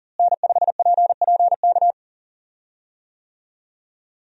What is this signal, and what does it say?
Morse code, heard as a keyed tone, about 39 wpm, 700 Hz. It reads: N5PPK
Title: USA Call Signs - 5 Characters - Repeated 3x - 40wpm